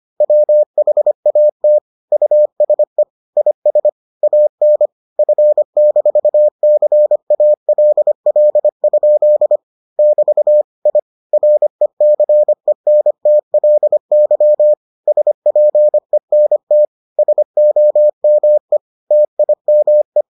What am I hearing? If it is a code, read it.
Morse code, heard as a keyed tone, about 25 wpm, 600 Hz. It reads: WHAT USE IS AN F-CALL? <BT> I RECENTLY SPENT SOME TIME